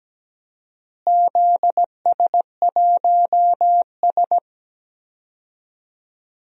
Morse code, heard as a keyed tone, about 17 wpm, 700 Hz. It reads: ZS1S
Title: DXCC-100 Call Signs - 4 Characters - Repeated 2x - 17wpm